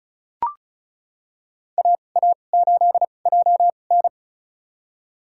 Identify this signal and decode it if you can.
Morse code, heard as a keyed tone, about 36 wpm, spaced as for 25 wpm, 700 Hz. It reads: AA8JN
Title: USA Call Signs - 5 Characters - 2x Character Spacing - 35wpm